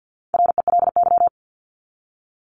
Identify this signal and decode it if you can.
Morse code, heard as a keyed tone, about 48 wpm, 700 Hz. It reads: RELY
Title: Top 1001-1250 Words 50wpm